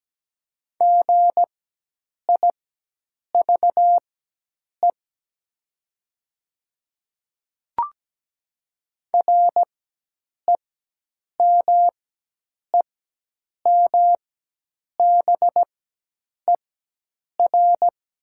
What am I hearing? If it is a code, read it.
Morse code, heard as a keyed tone, about 17 wpm, spaced as for 8 wpm, 700 Hz. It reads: GIVE REMEMBER
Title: 4x Character Spacing - Top 500 Words - 17wpm